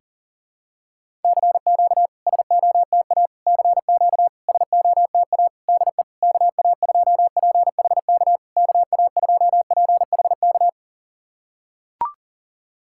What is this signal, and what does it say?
Morse code, heard as a keyed tone, about 40 wpm, 690 Hz. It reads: CQ SOTA CQ SOTA DE KA2PHK KA2PHK
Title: SOTA QSO Practice 40wpm